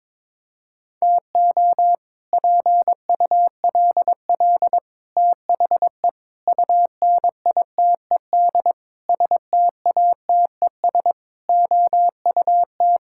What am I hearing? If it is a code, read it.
Morse code, heard as a keyed tone, about 22 wpm, 700 Hz. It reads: TO PULL THE UNITED STATES OUT
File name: practice / 22wpm / Sets of 8 Words - 22wpm